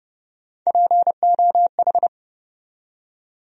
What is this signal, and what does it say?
Morse code, heard as a keyed tone, about 31 wpm, 700 Hz. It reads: POH
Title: USA Call Sign Postfixes - 3 Letters 30wpm